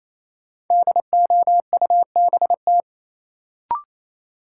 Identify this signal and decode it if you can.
Morse code, heard as a keyed tone, about 28 wpm, 700 Hz. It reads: DOUBT
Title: Sentences from Top 1500 Words - New Word Drill - 28wpm